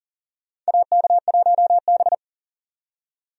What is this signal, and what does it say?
Morse code, heard as a keyed tone, about 40 wpm, 700 Hz. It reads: AK1B
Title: USA Call Signs - 4 Characters - 40wpm